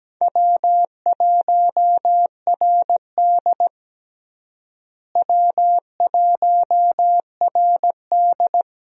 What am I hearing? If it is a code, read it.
Morse code, heard as a keyed tone, about 17 wpm, 700 Hz. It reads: W1RD W1RD
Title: USA Call Signs - 4 Characters - Repeated 2x - 17wpm